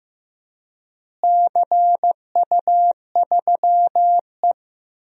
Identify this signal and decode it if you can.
Morse code, heard as a keyed tone, about 16 wpm, 700 Hz. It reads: CU3E
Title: DXCC-100 Call Signs - 4 Characters - Repeated 3x - 15wpm